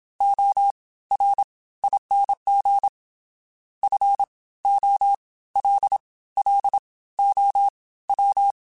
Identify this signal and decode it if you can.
Morse code, heard as a keyed tone, about 27 wpm, 800 Hz. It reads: O R ING F O L L O W